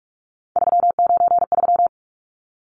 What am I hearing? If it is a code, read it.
Morse code, heard as a keyed tone, about 44 wpm, 700 Hz. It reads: 393
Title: Numbers - 3-Digit 45wpm